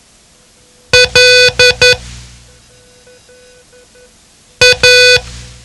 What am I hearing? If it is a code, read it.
Morse code, heard as a keyed tone, about 11 wpm, 500 Hz. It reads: L A
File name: ham radio / CodeQuick / Code Quick 07